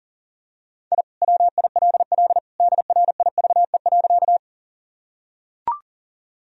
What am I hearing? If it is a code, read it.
Morse code, heard as a keyed tone, about 40 wpm, 700 Hz. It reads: I WILL DRIVE.
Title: Sentences from Top 1000 Words - Review Part 2 - 40wpm